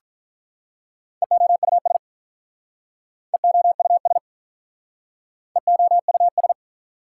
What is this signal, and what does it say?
Morse code, heard as a keyed tone, about 45 wpm, 690 Hz. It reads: EOUS EOUS EOUS